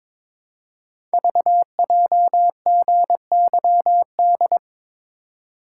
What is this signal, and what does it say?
Morse code, heard as a keyed tone, about 22 wpm, 700 Hz. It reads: VJGYD
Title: Code Groups - 5 Letters - 22wpm